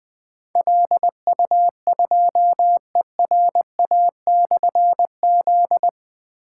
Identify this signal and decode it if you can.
Morse code, heard as a keyed tone, about 20 wpm, 700 Hz. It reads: LU2ERA/Z
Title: Difficult Call Signs 20wpm